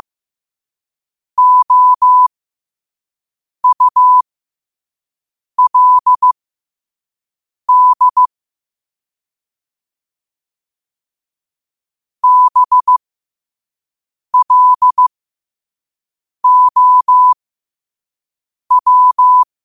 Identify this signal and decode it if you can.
Morse code, heard as a keyed tone, about 15 wpm, spaced as for 5 wpm, 1000 Hz. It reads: OULD BLOW
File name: qotdfiles / 5wpm